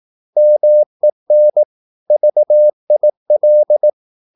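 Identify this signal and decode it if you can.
Morse code, heard as a keyed tone, about 18 wpm, 600 Hz. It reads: MEN VIL
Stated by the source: Slow (QRS) Morse practice at 18 wpm, CW med Gnister.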